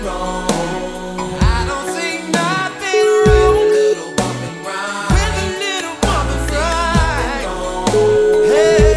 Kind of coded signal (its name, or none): telephone call progress tone